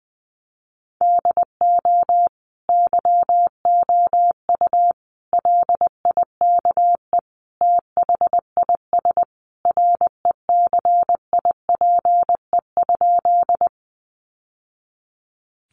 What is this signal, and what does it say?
Morse code, heard as a keyed tone, about 20 wpm, 700 Hz. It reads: DO YOU LIKE THIS RECIPE?
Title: Sentences from Top 1200 Words 20wpm